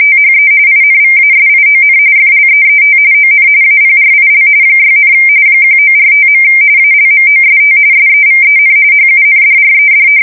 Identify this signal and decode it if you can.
RTTY, about 45.45 baud. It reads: THE INVASION FLEET ON APRIL 6, 1945. ALMOST SEVEN HUNDRED
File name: fsk170 / battleshipnj